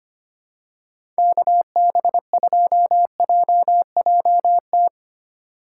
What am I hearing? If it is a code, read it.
Morse code, heard as a keyed tone, about 25 wpm, 700 Hz. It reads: KB2JJT
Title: USA Call Signs - 6 Characters - 4x Character Spacing - 25wpm